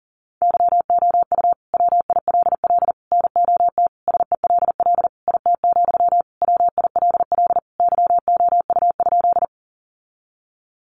Morse code, heard as a keyed tone, about 37 wpm, 700 Hz. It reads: YOU WILL NOT SELL IT, WILL YOU?
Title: Sentences from Top 1100 Words 40wpm